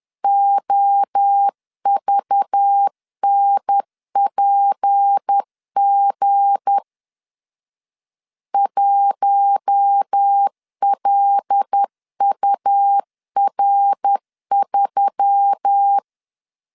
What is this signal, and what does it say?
Morse code, heard as a keyed tone, about 10 wpm, 800 Hz. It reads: OVNPG 1LUR3
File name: LEZIONI / 7 GF9Z / 7 G 10wpm 2501